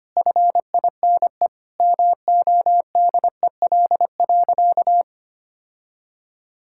Morse code, heard as a keyed tone, about 25 wpm, 700 Hz. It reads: FINE MODEL.
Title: Sentences from Top 1000 Words - Review Part 5 - 25wpm